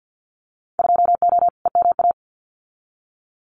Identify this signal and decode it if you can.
Morse code, heard as a keyed tone, about 50 wpm, 700 Hz. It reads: 2O EMA
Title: Winter Field Day Exchanges Encore 50wpm